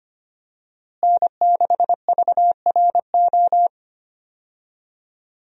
Morse code, heard as a keyed tone, about 25 wpm, 700 Hz. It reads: N6VRO
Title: USA Call Signs - 5 Characters - Repeated 3x - 25wpm